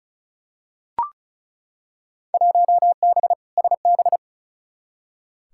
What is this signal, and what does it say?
Morse code, heard as a keyed tone, about 34 wpm, 700 Hz. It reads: 1B SB
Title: Field Day Exchanges - 35wpm